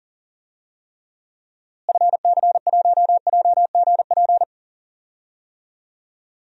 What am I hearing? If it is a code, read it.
Morse code, heard as a keyed tone, about 40 wpm, 700 Hz. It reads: FC1JGP
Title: DXCC-100 Call Signs - 6 Characters - Repeated 3x - 40wpm